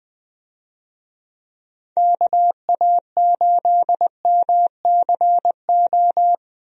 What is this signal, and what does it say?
Morse code, heard as a keyed tone, about 21 wpm, 700 Hz. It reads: KA8MCO
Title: USA Call Signs - 6 Characters - Repeated 3x - 20wpm